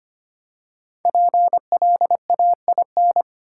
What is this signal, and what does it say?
Morse code, heard as a keyed tone, about 25 wpm, 700 Hz. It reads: PLAIN
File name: 5 Letter Words - First 3 Letters Given - 25wpm